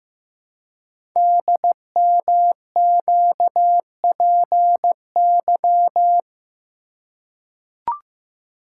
Morse code, heard as a keyed tone, about 15 wpm, 700 Hz. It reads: DMQPY
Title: Code Groups - 5 Letters - 15wpm